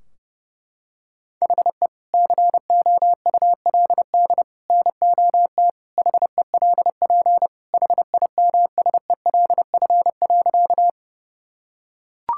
Morse code, heard as a keyed tone, about 30 wpm, 700 Hz. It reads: HE COULD NOT HELP HIMSELF.
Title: Sentences from Top 1600 Words 30wpm